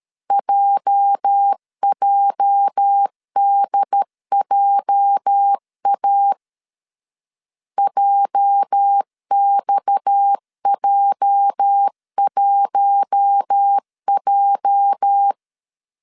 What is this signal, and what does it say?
Morse code, heard as a keyed tone, about 13 wpm, 800 Hz. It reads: JJDJA JXJ1J
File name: LEZIONI / 8 YJ45 / 8 YJ 12wpm 2235